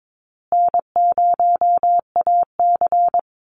Morse code, heard as a keyed tone, about 22 wpm, 700 Hz. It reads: N0AC
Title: Easy Call Signs Repeated 3x - 22wpm